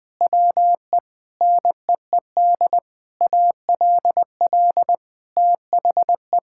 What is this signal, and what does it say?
Morse code, heard as a keyed tone, about 21 wpm, 700 Hz. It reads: WE NEED ALL THE
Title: Sentences with 8-Word Phrases - 20wpm